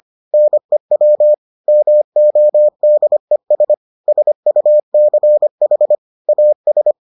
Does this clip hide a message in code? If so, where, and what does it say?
Yes: Morse code from 0.3 to 6.9 s, heard as a keyed tone, about 25 wpm, 600 Hz. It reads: NEW MODES SUCH AS